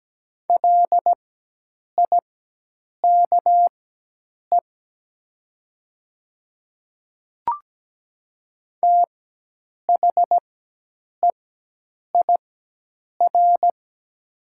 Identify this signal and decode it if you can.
Morse code, heard as a keyed tone, about 17 wpm, spaced as for 8 wpm, 700 Hz. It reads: LIKE THEIR
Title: 4x Character Spacing - Top 300 Words - 17wpm